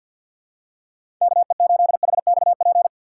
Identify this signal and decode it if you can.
Morse code, heard as a keyed tone, about 52 wpm, 700 Hz. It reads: KE8HXP
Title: USA Call Signs - 2x Character Spacing - 50wpm